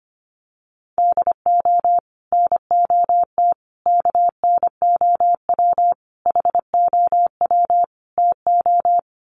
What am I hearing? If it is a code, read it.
Morse code, heard as a keyed tone, about 25 wpm, 700 Hz. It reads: DO NOT KNOW HOW TO
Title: Sets of 5 Words 25wpm